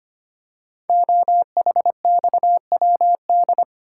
Morse code, heard as a keyed tone, about 25 wpm, 700 Hz. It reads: OHXWD
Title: Code Groups - 5 Letters - 25wpm